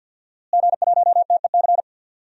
Morse code, heard as a keyed tone, about 47 wpm, 700 Hz. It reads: G1TEC
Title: DXCC-100 Call Signs - 5 Characters - Repeated 3x - 50wpm